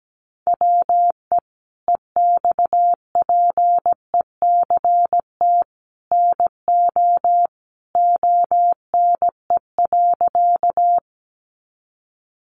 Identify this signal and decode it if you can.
Morse code, heard as a keyed tone, about 17 wpm, 700 Hz. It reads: WE EXPECT NO ONE.
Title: Sentences from Top 1100 Words 17wpm